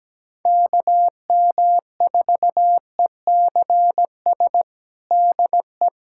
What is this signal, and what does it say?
Morse code, heard as a keyed tone, about 17 wpm, 700 Hz. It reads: KM4ECS DE
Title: USA Callsign DE Callsign - 6 Characters - 17wpm